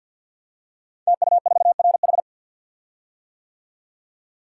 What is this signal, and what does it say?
Morse code, heard as a keyed tone, about 49 wpm, 700 Hz. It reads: TU4RH
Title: Code Groups - 5 Letter:Numbers - 50wpm